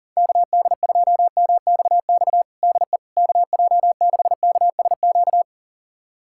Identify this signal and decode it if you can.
Morse code, heard as a keyed tone, about 42 wpm, 700 Hz. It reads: KD2MXX DE KJ6KSQ
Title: USA Callsign DE Callsign 40wpm